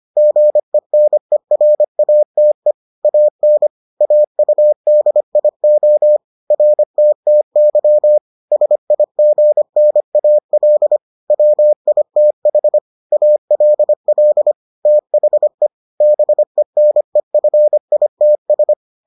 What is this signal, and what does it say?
Morse code, heard as a keyed tone, about 25 wpm, 600 Hz. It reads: GENERATE AN AUDIO RTTY SIGNAL WITH ALL THE BENEFITS